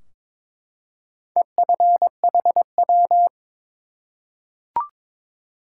Morse code, heard as a keyed tone, about 22 wpm, 700 Hz. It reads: EFHW
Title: Top 180 QSO Elements 22wpm